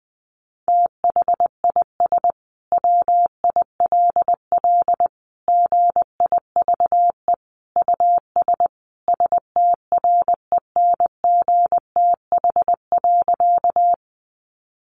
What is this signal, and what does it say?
Morse code, heard as a keyed tone, about 20 wpm, 700 Hz. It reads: THIS WILL GIVE US STRENGTH.
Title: Sentences from Top 1300 Words 20wpm